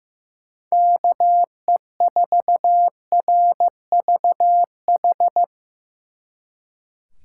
Morse code, heard as a keyed tone, about 17 wpm, 700 Hz. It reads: KE4RVH
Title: Speed Racing - USA 2x3 Call Signs 15wpm